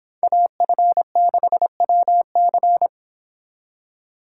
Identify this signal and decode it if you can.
Morse code, heard as a keyed tone, about 30 wpm, 700 Hz. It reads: AF6WC